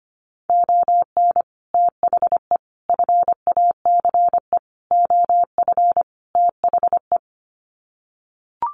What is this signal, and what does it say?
Morse code, heard as a keyed tone, about 25 wpm, 700 Hz. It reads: ON THE FACE OF THE
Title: Sets of 5 Words Encore2 25wpm